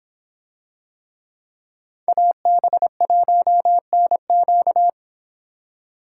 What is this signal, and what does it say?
Morse code, heard as a keyed tone, about 25 wpm, 700 Hz. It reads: AB1NQ